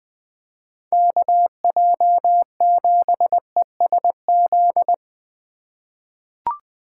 Morse code, heard as a keyed tone, about 20 wpm, 700 Hz. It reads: KJ7ESZ
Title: USA Call Signs - 6 Characters - 3x Character Spacing - 20wpm